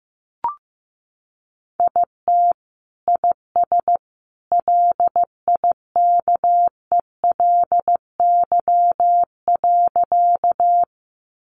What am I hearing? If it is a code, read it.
Morse code, heard as a keyed tone, about 15 wpm, 700 Hz. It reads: IT IS LIKELY.
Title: Sentences from Top 600 Words 15wpm